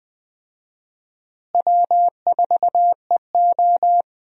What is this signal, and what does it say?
Morse code, heard as a keyed tone, about 20 wpm, 700 Hz. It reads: W4EO
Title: USA Call Signs - 4 Characters - Repeated 3x - 20wpm